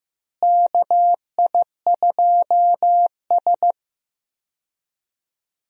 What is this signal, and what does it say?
Morse code, heard as a keyed tone, about 15 wpm, 700 Hz. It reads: KI2S
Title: USA Call Signs - 4 Characters - Repeated 2x - 15wpm